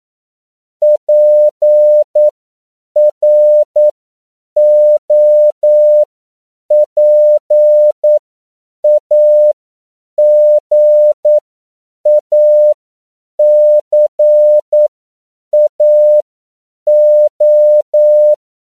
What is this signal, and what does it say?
Morse code, heard as a keyed tone, about 9 wpm, spaced as for 7 wpm, 600 Hz. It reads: PROPAGACAO
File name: CW / 10PPM